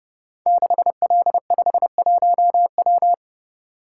Morse code, heard as a keyed tone, about 30 wpm, 700 Hz. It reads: 6L51W
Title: Code Groups - 5 Letter:Numbers - 30wpm